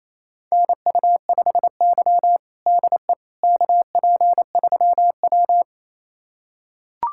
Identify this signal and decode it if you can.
Morse code, heard as a keyed tone, about 28 wpm, 700 Hz. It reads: NU5Y DE KP3W
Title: USA Callsign DE Callsign - 4 Characters - 28wpm